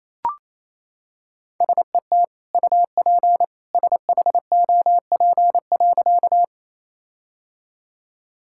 Morse code, heard as a keyed tone, about 28 wpm, 700 Hz. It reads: SET UP SHOP.
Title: Sentences from Top 1600 Words 28wpm